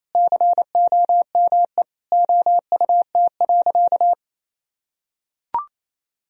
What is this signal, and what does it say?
Morse code, heard as a keyed tone, about 28 wpm, 700 Hz. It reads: COME OUT.
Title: Sentences from Top 1000 Words - Review Part 1 - 28wpm